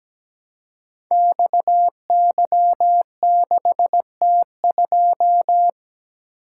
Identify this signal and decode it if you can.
Morse code, heard as a keyed tone, about 17 wpm, 700 Hz. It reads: XY6T2
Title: Code Groups - 5 Letter:Numbers - 17wpm